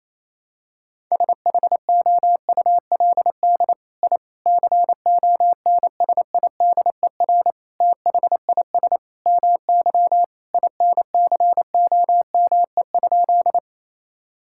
Morse code, heard as a keyed tone, about 28 wpm, 700 Hz. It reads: SHOULD I CONSIDER THIS MY INCOME?
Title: Sentences from Top 1000 Words - Review Part 5 - 28wpm